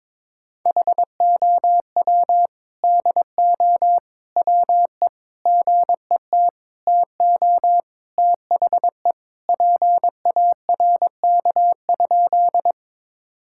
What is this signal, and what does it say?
Morse code, heard as a keyed tone, about 22 wpm, 700 Hz. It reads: HOW DO WE GET TO THE PARK?
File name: practice / 22wpm / Sentences with 4-Word Phrases - 22wpm